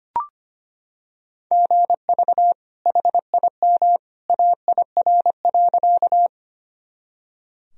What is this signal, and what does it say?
Morse code, heard as a keyed tone, about 25 wpm, 700 Hz. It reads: GV HIM AIR.
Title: Sentences from Top 200 Words and Abbreviations 25wpm